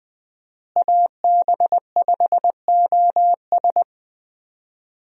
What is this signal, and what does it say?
Morse code, heard as a keyed tone, about 20 wpm, 700 Hz. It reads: AB5OS
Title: USA Call Signs - 5 Characters - Repeated 2x - 20wpm